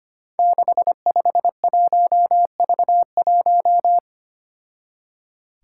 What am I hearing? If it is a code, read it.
Morse code, heard as a keyed tone, about 25 wpm, 700 Hz. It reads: 651V1
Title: Five Challenging Letter-Number Combinations 25wpm